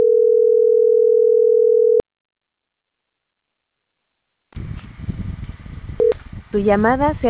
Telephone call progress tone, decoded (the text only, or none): ringback tone